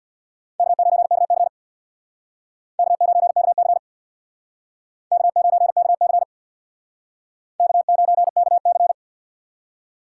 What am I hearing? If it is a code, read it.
Morse code, heard as a keyed tone, about 62 wpm, 700 Hz. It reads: K9KC K9KC K9KC K9KC
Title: Speed Racing - USA 1x2 Call Signs 50wpm